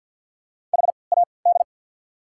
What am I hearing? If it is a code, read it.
Morse code, heard as a keyed tone, about 54 wpm, spaced as for 29 wpm, 690 Hz. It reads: HAD
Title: 3x Character Spacing - Top 100 Words - 50wpm